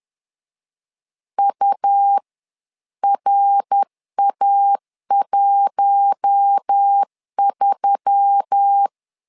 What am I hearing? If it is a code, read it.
Morse code, heard as a keyed tone, about 10 wpm, spaced as for 9 wpm, 800 Hz. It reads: URA13
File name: LEZIONI / 4 Q1S3 10wpm 2427